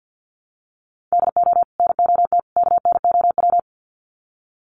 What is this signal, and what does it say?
Morse code, heard as a keyed tone, about 50 wpm, 700 Hz. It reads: DO NOT KNOW
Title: Sets of 3 Words 50wpm